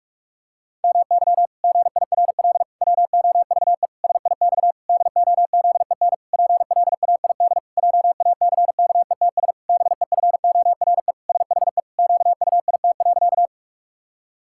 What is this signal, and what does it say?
Morse code, heard as a keyed tone, about 41 wpm, 700 Hz. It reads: MY GIRL WOVE SIX DOZEN PLAID JACKETS BEFORE SHE QUIT.
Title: Pangram Challenge v2 - 45wpm